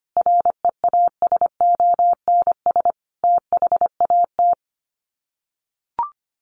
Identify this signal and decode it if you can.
Morse code, heard as a keyed tone, about 25 wpm, 700 Hz. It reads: REASONS THAT
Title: Sets of 5 Words Encore 25wpm